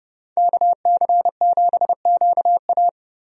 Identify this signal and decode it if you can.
Morse code, heard as a keyed tone, about 30 wpm, 700 Hz. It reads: KC7QA
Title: USA Call Signs - 5 Characters - 30wpm